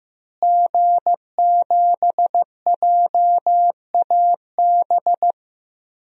Morse code, heard as a keyed tone, about 15 wpm, 700 Hz. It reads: G7JAB
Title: DXCC-100 Call Signs - 5 Characters - Repeated 2x - 15wpm